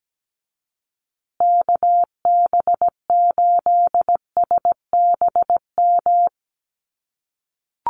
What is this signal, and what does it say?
Morse code, heard as a keyed tone, about 17 wpm, 700 Hz. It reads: KB8SBM
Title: USA Call Signs Repeated 2x 17wpm